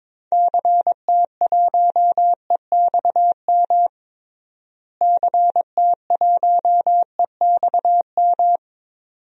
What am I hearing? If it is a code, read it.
Morse code, heard as a keyed tone, about 22 wpm, 700 Hz. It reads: CT1EXM CT1EXM
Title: DXCC-100 Call Signs - 6 Characters - Repeated 2x - 22wpm